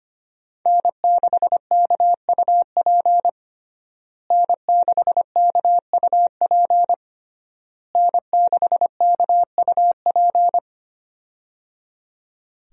Morse code, heard as a keyed tone, about 25 wpm, 700 Hz. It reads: N6KUP N6KUP N6KUP
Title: USA Call Signs - 5 Characters - Repeated 3x - 25wpm